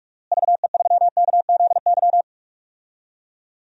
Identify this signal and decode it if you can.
Morse code, heard as a keyed tone, about 46 wpm, 700 Hz. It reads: VE3KZY
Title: DXCC-100 Call Signs - 6 Characters - Repeated 2x - 45wpm